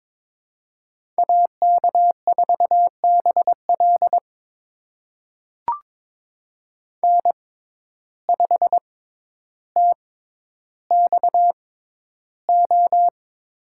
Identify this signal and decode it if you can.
Morse code, heard as a keyed tone, about 22 wpm, spaced as for 11 wpm, 700 Hz. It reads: AK4BL N5TXO
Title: USA Call Signs - 5 Characters - 6x Character Spacing - 22wpm